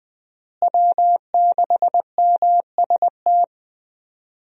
Morse code, heard as a keyed tone, about 20 wpm, 700 Hz. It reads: W6MST
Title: USA Call Signs - 5 Characters - Repeated 3x - 20wpm